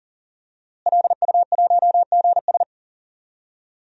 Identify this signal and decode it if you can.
Morse code, heard as a keyed tone, about 40 wpm, 700 Hz. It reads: LU1GS
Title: DXCC-100 Call Signs - 5 Characters - Repeated 3x - 40wpm